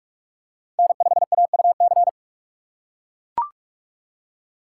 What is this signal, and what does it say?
Morse code, heard as a keyed tone, about 44 wpm, 700 Hz. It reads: N5AUC